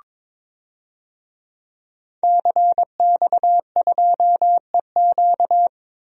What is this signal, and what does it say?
Morse code, heard as a keyed tone, about 22 wpm, 700 Hz. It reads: CX2EQ
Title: DXCC-100 Call Signs - 5 Characters - 22wpm